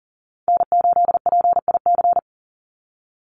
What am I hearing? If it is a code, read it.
Morse code, heard as a keyed tone, about 40 wpm, 700 Hz. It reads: N8PIC
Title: USA Call Signs 40wpm